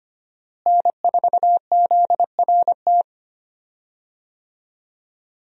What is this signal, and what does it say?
Morse code, heard as a keyed tone, about 25 wpm, 700 Hz. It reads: N4ZRT
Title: USA Call Signs - 5 Characters - Repeated 3x - 25wpm